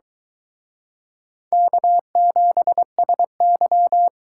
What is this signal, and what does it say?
Morse code, heard as a keyed tone, about 23 wpm, 700 Hz. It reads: K7SY